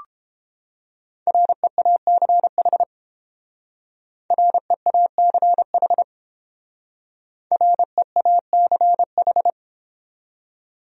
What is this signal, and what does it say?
Morse code, heard as a keyed tone, about 29 wpm, 700 Hz. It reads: REACH REACH REACH